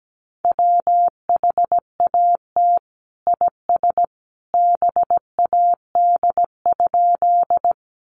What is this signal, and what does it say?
Morse code, heard as a keyed tone, about 17 wpm, 700 Hz. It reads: WHAT IS BAD?
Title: Sentences from Top 700 Words 17wpm